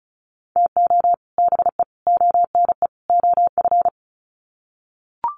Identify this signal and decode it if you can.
Morse code, heard as a keyed tone, about 34 wpm, 700 Hz. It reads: TO BE ONE OF
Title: Sets of 4 Words 35wpm